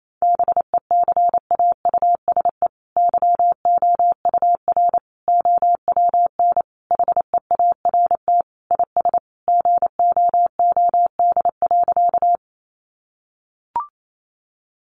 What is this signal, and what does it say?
Morse code, heard as a keyed tone, about 28 wpm, 700 Hz. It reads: BECAUSE YOUR OWN HEART IS GOOD.
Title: Sentences from Top 1200 Words 28wpm